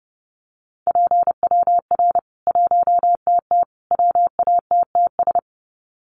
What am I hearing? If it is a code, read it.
Morse code, heard as a keyed tone, about 30 wpm, 700 Hz. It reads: PWR 1TT WATTS
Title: Common QSO Phrases 30wpm